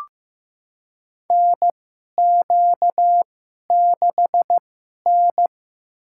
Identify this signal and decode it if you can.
Morse code, heard as a keyed tone, about 15 wpm, spaced as for 11 wpm, 700 Hz. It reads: NQ6N
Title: USA Call Signs - 4 Characters - 2x Character Spacing - 15wpm